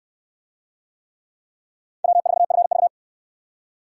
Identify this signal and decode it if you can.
Morse code, heard as a keyed tone, about 68 wpm, 700 Hz. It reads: W4FV